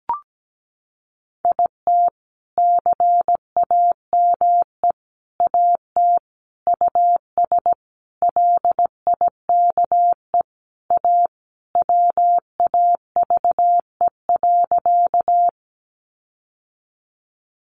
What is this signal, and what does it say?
Morse code, heard as a keyed tone, about 17 wpm, 700 Hz. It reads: IT CAME AT US LIKE A WAVE.
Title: Sentences from Top 1400 Words 17wpm